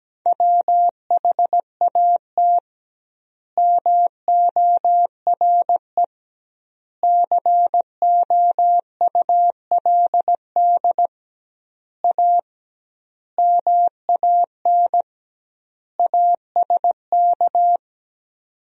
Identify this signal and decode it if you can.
Morse code, heard as a keyed tone, about 17 wpm, 700 Hz. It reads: WHAT MORE COULD A MAN ASK
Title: 2x Word-Spacing - Sentences from Top 200 Words - 17wpm